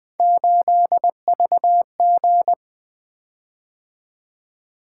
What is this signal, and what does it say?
Morse code, heard as a keyed tone, about 20 wpm, 700 Hz. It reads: 8VG
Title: Three Letter-Number Combinations - 20wpm